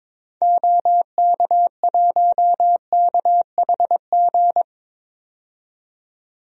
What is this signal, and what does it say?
Morse code, heard as a keyed tone, about 22 wpm, 700 Hz. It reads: OK1KHG
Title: DXCC-100 Call Signs - 6 Characters - 22wpm